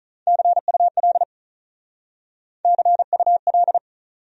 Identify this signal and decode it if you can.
Morse code, heard as a keyed tone, about 47 wpm, 700 Hz. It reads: CUL CUL